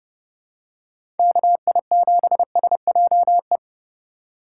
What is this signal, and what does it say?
Morse code, heard as a keyed tone, about 30 wpm, 700 Hz. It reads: KI7SJE
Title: USA Call Signs - 3x Character Spacing - 30wpm